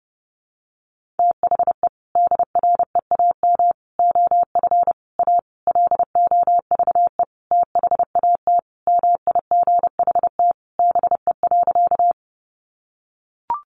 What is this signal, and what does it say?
Morse code, heard as a keyed tone, about 30 wpm, 700 Hz. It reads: THE DREAM OF A LOVE THAT MIGHT BE.
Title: Sentences from Top 1500 Words 30wpm